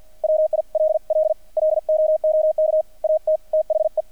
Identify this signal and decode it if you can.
Morse code, heard as a keyed tone, about 50 wpm, 620 Hz. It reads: WILL LOOK AT THE
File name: archived / CRSnet